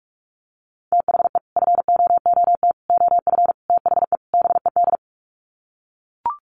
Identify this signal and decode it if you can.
Morse code, heard as a keyed tone, about 44 wpm, 700 Hz. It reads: THE FOOT OF THE BED
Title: Sets of 5 Words Encore2 45wpm